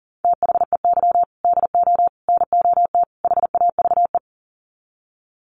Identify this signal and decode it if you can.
Morse code, heard as a keyed tone, about 40 wpm, 700 Hz. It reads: THEY DO NOT HAVE
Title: Sets of 4 Words 40wpm